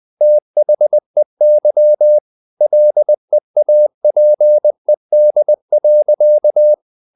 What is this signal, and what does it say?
Morse code, heard as a keyed tone, about 20 wpm, 600 Hz. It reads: THEY LEAPED.
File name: the old man and the sea part 2 20wpm